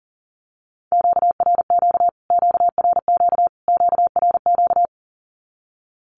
Morse code, heard as a keyed tone, about 40 wpm, 700 Hz. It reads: QRQ QRQ QRQ
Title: Q-Codes Repeated 3x 40wpm